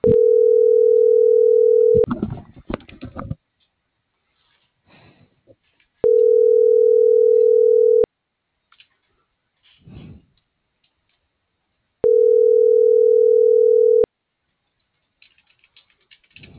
Ringback tone (North America), 440 + 480 Hz.